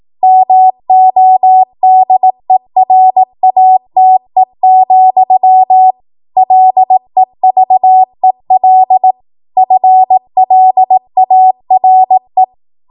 Morse code, heard as a keyed tone, about 18 wpm, 750 Hz. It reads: MODERATE, LEVEL FLARE
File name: Morse / 18WPM